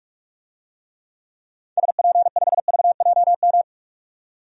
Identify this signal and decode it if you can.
Morse code, heard as a keyed tone, about 42 wpm, 700 Hz. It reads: SP5VJM
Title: DXCC-100 Call Signs - 6 Characters - 45wpm